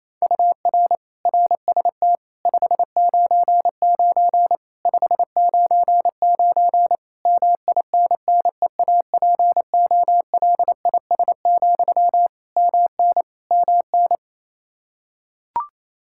Morse code, heard as a keyed tone, about 28 wpm, 700 Hz. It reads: UR RST 599 599 MINNEAPOLIS, MN MN
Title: Practice QSO - Exchange 1 - Beginning - 28wpm